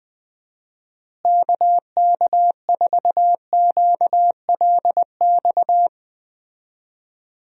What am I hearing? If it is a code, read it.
Morse code, heard as a keyed tone, about 20 wpm, 700 Hz. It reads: KK4QLX
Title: USA Call Signs - 6 Characters - Repeated 3x - 20wpm